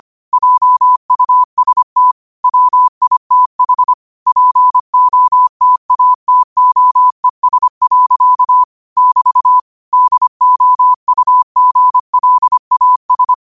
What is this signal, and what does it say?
Morse code, heard as a keyed tone, about 25 wpm, 1000 Hz. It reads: JUST WITH POTATOES. <BT> DOUGLAS